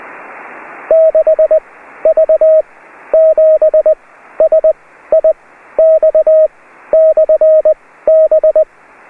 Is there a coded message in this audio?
Yes: Morse code from 0.9 to 8.6 s, heard as a keyed tone, about 20 wpm, spaced as for 13 wpm, 630 Hz. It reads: 6V7SIX/B